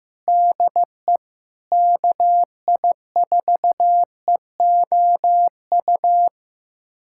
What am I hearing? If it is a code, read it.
Morse code, heard as a keyed tone, about 15 wpm, 700 Hz. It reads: DE KI4EOU
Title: USA Callsign DE Callsign - 6 Characters - 15wpm